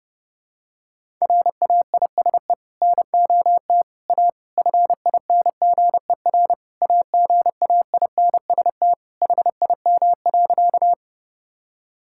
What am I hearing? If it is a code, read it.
Morse code, heard as a keyed tone, about 29 wpm, 700 Hz. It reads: RAISE NOT A FINGER AGAINST HIM.
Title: Sentences from Top 1600 Words 30wpm